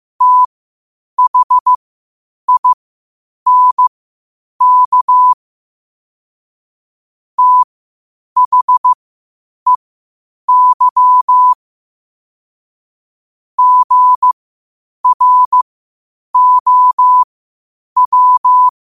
Morse code, heard as a keyed tone, about 15 wpm, spaced as for 8 wpm, 1000 Hz. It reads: THINK THEY GROW
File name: qotdfiles / 8wpm